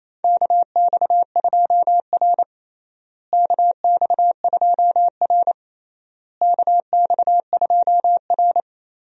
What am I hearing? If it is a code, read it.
Morse code, heard as a keyed tone, about 28 wpm, 700 Hz. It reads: KX2R KX2R KX2R
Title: USA Call Signs - 4 Characters - Repeated 3x - 28wpm